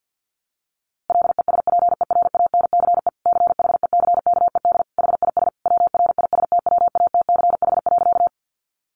Morse code, heard as a keyed tone, about 50 wpm, 700 Hz. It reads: LESPERANCE CHECKED HIS WRISTWATCH.
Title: A Sound Of Thunder 50wpm